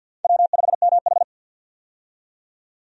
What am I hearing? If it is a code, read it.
Morse code, heard as a keyed tone, about 50 wpm, 700 Hz. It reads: W5MH
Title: USA Call Signs - 4 Characters - 50wpm